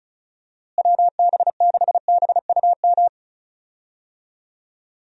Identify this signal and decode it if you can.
Morse code, heard as a keyed tone, about 36 wpm, 700 Hz. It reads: WB6BUM
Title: USA Call Signs - 6 Characters - Repeated 2x - 35wpm